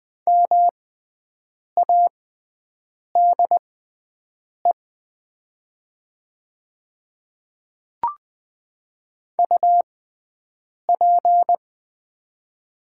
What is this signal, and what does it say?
Morse code, heard as a keyed tone, about 20 wpm, spaced as for 7 wpm, 700 Hz. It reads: MADE UP
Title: 6x Character Spacing - Top 500 Words - 20wpm